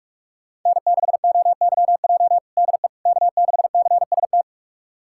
Morse code, heard as a keyed tone, about 43 wpm, 700 Hz. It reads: N6OYJ DE K6CST
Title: USA Callsign DE Callsign - 5 Characters - 45wpm